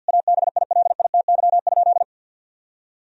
Morse code, heard as a keyed tone, about 50 wpm, 700 Hz. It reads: ABILITY?
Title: Sentences from Top 1000 Words - Review Part 4 - 50wpm